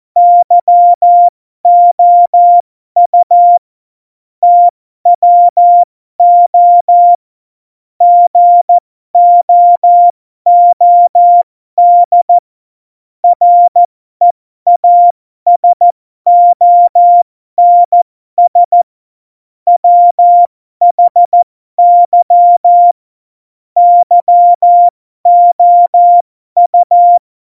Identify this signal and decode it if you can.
Morse code, heard as a keyed tone, about 14 wpm, spaced as for 12 wpm, 700 Hz. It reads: YOU TWO GOOD REASONS WHY YOU
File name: Morsekurs / Witze